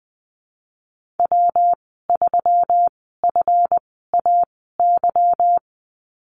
Morse code, heard as a keyed tone, about 20 wpm, spaced as for 15 wpm, 700 Hz. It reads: W3FAY
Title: 173 Lesson 28g - Number 9 - Call Signs 20wpm